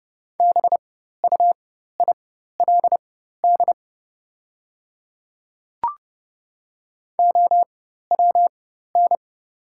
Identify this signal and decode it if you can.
Morse code, heard as a keyed tone, about 30 wpm, spaced as for 14 wpm, 700 Hz. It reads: BUILD OWN
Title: 4x Character Spacing - Top 500 Words - 30wpm